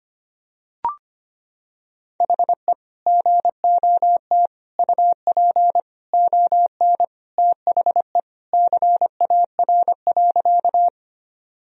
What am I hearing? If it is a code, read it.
Morse code, heard as a keyed tone, about 25 wpm, 700 Hz. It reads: HE GOT UP ON THE CAR.
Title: Sentences from Top 1000 Words - Review Part 1 - 25wpm